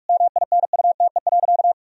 Morse code, heard as a keyed tone, about 46 wpm, 700 Hz. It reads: MINUTE.
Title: Sentences from Top 1000 Words - Review Part 1 - 45wpm